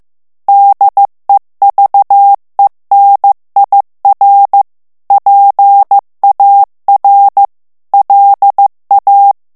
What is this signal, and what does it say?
Morse code, heard as a keyed tone, about 15 wpm, 800 Hz. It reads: DEVENIR PAR LA
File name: morse mp3 / FRANCAIS TEXTE CLAIR 2 VITESSE 14 MOTS MINUTES